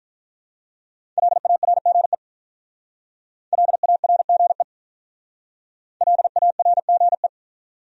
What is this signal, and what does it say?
Morse code, heard as a keyed tone, about 44 wpm, 700 Hz. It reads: LARGE LARGE LARGE
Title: Speed-Racing Top 500 Words 35wpm